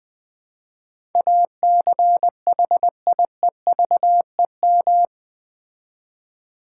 Morse code, heard as a keyed tone, about 20 wpm, 700 Hz. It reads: ACHIEVEM
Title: Build A Word - 4 Letters At A Time - 11-Letter Words - 20wpm